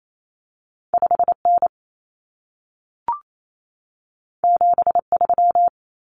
Morse code, heard as a keyed tone, about 28 wpm, 700 Hz. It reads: 5N 73
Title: Common Cut Numbers - 2-Digit 28wpm